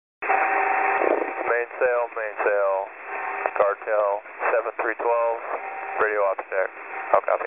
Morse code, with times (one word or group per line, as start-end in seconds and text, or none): none